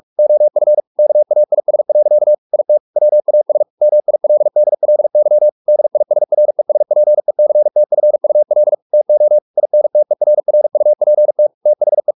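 Morse code, heard as a keyed tone, about 44 wpm, 600 Hz. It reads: OF KAIS. IT WAS MILDLY DISRESPECTFUL TO INTERRUPT THE
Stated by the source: Morse practice at 45 wpm, the novel Never by Ken Follett in English, from CW med Gnister.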